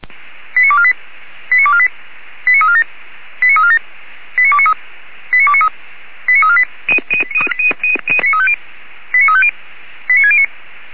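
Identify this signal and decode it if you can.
Five-tone selective call (ZVEI), heard as quick runs of five tones, tones of about 70 ms. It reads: 89248 89248 89347 89347 89293 89293 89348 89350 89609
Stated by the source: ZVEI five-tone call received by SDR, recorded at feuerwehr-orlamuende.org fire brigade.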